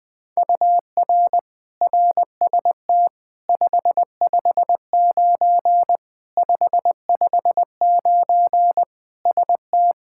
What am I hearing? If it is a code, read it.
Morse code, heard as a keyed tone, about 20 wpm, 700 Hz. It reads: UR RST 559 559 ST
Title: Practice QSO - Exchange 1 - Beginning - 20wpm